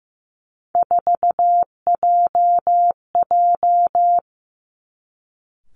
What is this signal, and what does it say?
Morse code, heard as a keyed tone, about 15 wpm, 700 Hz. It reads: 4JJ
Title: Three Challenging Letter-Number Combinations 15wpm